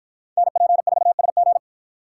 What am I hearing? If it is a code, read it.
Morse code, heard as a keyed tone, about 53 wpm, 700 Hz. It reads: NP4SG